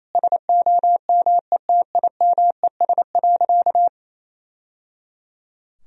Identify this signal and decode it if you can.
Morse code, heard as a keyed tone, about 28 wpm, 700 Hz. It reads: SOMETIMES.
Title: Sentences from Top 1000 Words - Review Part 3 - 28wpm